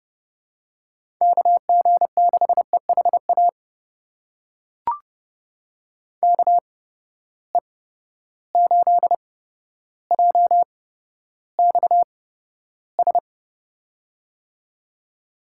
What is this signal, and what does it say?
Morse code, heard as a keyed tone, about 31 wpm, spaced as for 13 wpm, 700 Hz. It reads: KG6EHA KE8JXS
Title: USA Call Signs - 6 Characters - 8x Character Spacing - 30wpm